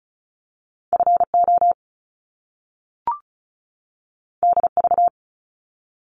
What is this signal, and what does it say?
Morse code, heard as a keyed tone, about 36 wpm, 700 Hz. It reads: FO DV
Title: Top Two Letter Combinations in Words 35wpm